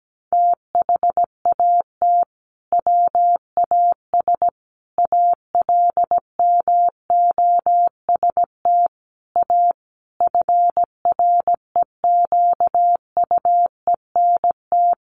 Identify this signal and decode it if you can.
Morse code, heard as a keyed tone, about 17 wpm, 700 Hz. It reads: THAT WAS ALMOST A FREQUENT
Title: Sentences from Top 1400 Words 17wpm